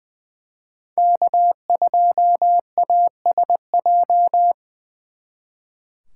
Morse code, heard as a keyed tone, about 24 wpm, 700 Hz. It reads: K2ASJ
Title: Speed Racing - USA 1x3 Call Signs 20wpm